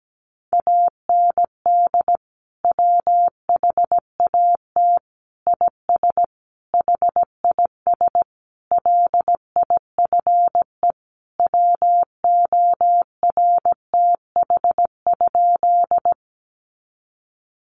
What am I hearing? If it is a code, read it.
Morse code, heard as a keyed tone, about 17 wpm, 700 Hz. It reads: AND WHAT IS HIS LIFE WORTH?
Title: Sentences from Top 800 Words 17wpm